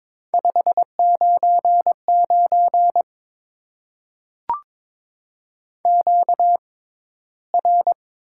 Morse code, heard as a keyed tone, about 22 wpm, spaced as for 13 wpm, 700 Hz. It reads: 599 QR
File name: Top 90 QSO Elements - 6x Character Spacing - 22wpm